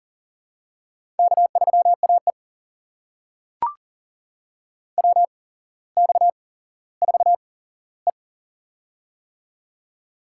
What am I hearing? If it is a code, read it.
Morse code, heard as a keyed tone, about 40 wpm, spaced as for 17 wpm, 700 Hz. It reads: K3AE WX4E
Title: USA Call Signs - 4 Characters - 8x Character Spacing - 40wpm